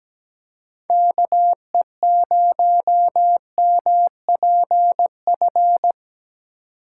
Morse code, heard as a keyed tone, about 17 wpm, 700 Hz. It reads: KE0MPF
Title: USA Call Signs - 4x Character Spacing - 17wpm